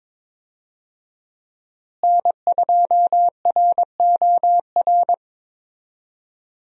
Morse code, heard as a keyed tone, about 22 wpm, 700 Hz. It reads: N2ROR
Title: USA Call Signs - 5 Characters - Repeated 2x - 22wpm